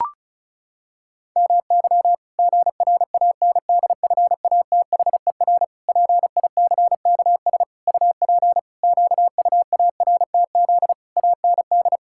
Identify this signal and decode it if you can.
Morse code, heard as a keyed tone, about 36 wpm, 700 Hz. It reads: MY GRANDFATHER PICKS UP QUARTZ AND
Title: Pangram Challenge v2 - 35wpm